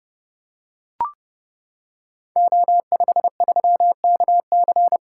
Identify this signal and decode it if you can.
Morse code, heard as a keyed tone, about 30 wpm, 700 Hz. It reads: O53KC
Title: Code Groups - 5 Letter:Numbers - 30wpm